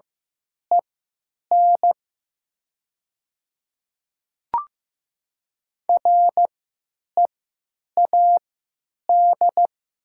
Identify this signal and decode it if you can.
Morse code, heard as a keyed tone, about 15 wpm, spaced as for 9 wpm, 700 Hz. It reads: EN READ
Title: 3x Character Spacing - Top 200 Words - 15wpm